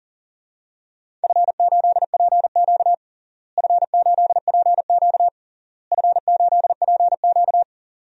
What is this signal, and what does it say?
Morse code, heard as a keyed tone, about 37 wpm, 700 Hz. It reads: F8PQ F8PQ F8PQ
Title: DXCC-100 Call Signs - 4 Characters - Repeated 3x - 40wpm